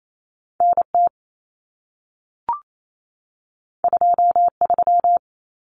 Morse code, heard as a keyed tone, about 28 wpm, 700 Hz. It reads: NT 23
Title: Common Cut Numbers - 2-Digit 28wpm